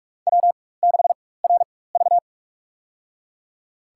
Morse code, heard as a keyed tone, about 43 wpm, spaced as for 21 wpm, 700 Hz. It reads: W6RV
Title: USA Call Signs - 4 Characters - 4x Character Spacing - 45wpm